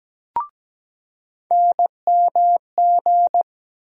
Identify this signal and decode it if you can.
Morse code, heard as a keyed tone, about 17 wpm, 700 Hz. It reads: NMG
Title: USA Call Sign Postfixes - 3 Letters 17wpm